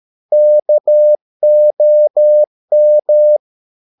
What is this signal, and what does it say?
Morse code, heard as a keyed tone, about 13 wpm, 600 Hz. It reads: KOM
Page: Ferie 13 wpm